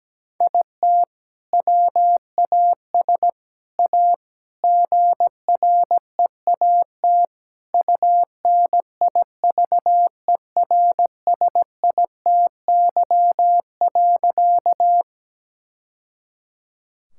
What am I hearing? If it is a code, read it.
Morse code, heard as a keyed tone, about 17 wpm, 700 Hz. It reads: IT WAS A GREAT UNIVERSITY.
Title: Sentences from Top 1000 Words - Review Part 5 - 17wpm